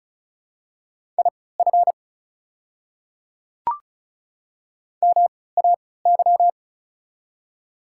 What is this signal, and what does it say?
Morse code, heard as a keyed tone, about 34 wpm, spaced as for 20 wpm, 700 Hz. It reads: IF MAY
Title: 3x Character Spacing - Top 300 Words - 35wpm